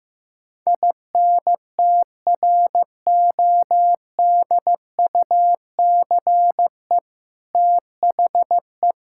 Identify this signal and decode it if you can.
Morse code, heard as a keyed tone, about 15 wpm, 700 Hz. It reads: INTRODUCE THE
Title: Sentences from Top 1600 Words 15wpm